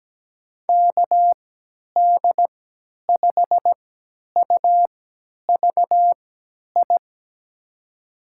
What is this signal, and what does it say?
Morse code, heard as a keyed tone, about 17 wpm, spaced as for 10 wpm, 700 Hz. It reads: KD5UVI
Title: USA Call Signs - 6 Characters - 3x Character Spacing - 17wpm